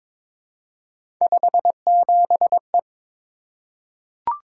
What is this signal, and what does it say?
Morse code, heard as a keyed tone, about 22 wpm, 700 Hz. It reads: 57E
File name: practice / 22wpm / Three Letter-Number Combinations - 22wpm